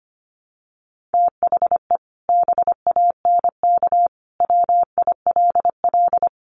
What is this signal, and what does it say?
Morse code, heard as a keyed tone, about 25 wpm, 700 Hz. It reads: THE BANK WILL
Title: Sentences from Top 1500 Words 25wpm